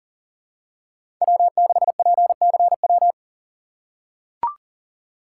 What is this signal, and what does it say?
Morse code, heard as a keyed tone, about 43 wpm, 690 Hz. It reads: W6PCW
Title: USA Call Signs - 4x Character Spacing - 40wpm